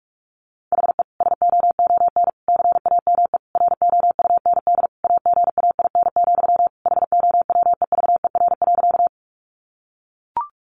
Morse code, heard as a keyed tone, about 44 wpm, 700 Hz. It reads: HE SOON CAME ROUND AGAIN, HOWEVER.
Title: Sentences from Top 600 Words 45wpm